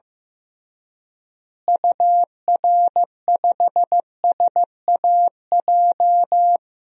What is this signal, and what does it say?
Morse code, heard as a keyed tone, about 15 wpm, 700 Hz. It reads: UR5SAJ
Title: DXCC-100 Call Signs - 6 Characters - Repeated 3x - 15wpm